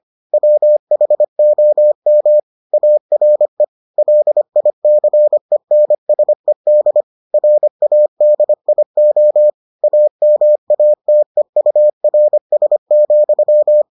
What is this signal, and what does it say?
Morse code, heard as a keyed tone, about 25 wpm, 600 Hz. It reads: WHOM ARE LICENSED RADIO AMATEURS,